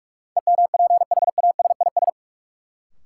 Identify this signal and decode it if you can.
Morse code, heard as a keyed tone, about 46 wpm, 700 Hz. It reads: EMPHASIS
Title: Sentences from Top 1400 Words - New Word Drill Alt - 45wpm